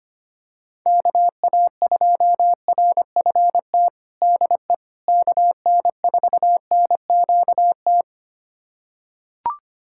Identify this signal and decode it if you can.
Morse code, heard as a keyed tone, about 25 wpm, 700 Hz. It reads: KA2RFT DE KN4NQT
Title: USA Callsign DE Callsign 25wpm